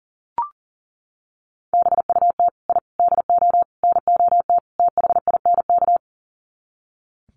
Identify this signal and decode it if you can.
Morse code, heard as a keyed tone, about 40 wpm, 700 Hz. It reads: BUT I DO NOT THINK
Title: Sets of 5 Words 40wpm